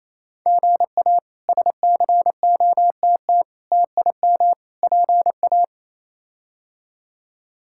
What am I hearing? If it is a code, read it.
Morse code, heard as a keyed tone, about 28 wpm, 700 Hz. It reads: GA SCOTT TIM PA
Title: K1USN SST QSO Practice - 28wpm